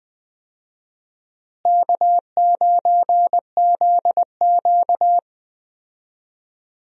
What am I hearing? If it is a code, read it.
Morse code, heard as a keyed tone, about 20 wpm, 700 Hz. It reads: K9ZQ
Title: USA Call Signs - 4 Characters - Repeated 2x - 20wpm